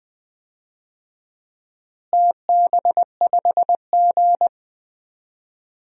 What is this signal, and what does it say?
Morse code, heard as a keyed tone, about 20 wpm, 700 Hz. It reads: TB5G
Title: DXCC-100 Call Signs - 4 Characters - Repeated 2x - 20wpm